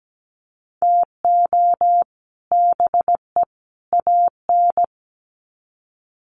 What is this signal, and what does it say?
Morse code, heard as a keyed tone, about 17 wpm, 700 Hz. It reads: TO BE AN
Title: Sets of 3 Words 17wpm